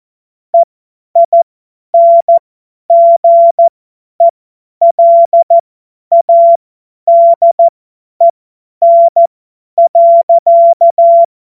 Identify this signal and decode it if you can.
Morse code, heard as a keyed tone, about 14 wpm, spaced as for 10 wpm, 680 Hz. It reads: EINGELADEN.
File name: Morsekurs / Witze / Witz 23. August 2022 50 F70